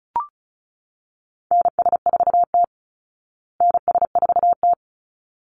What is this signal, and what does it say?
Morse code, heard as a keyed tone, about 34 wpm, 700 Hz. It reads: NS4T NS4T
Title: Call Signs Repeated 3x - 35wpm